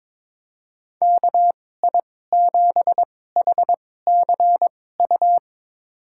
Morse code, heard as a keyed tone, about 22 wpm, spaced as for 16 wpm, 700 Hz. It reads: KI7HCU
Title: USA Call Signs - 6 Characters - 2x Character Spacing - 22wpm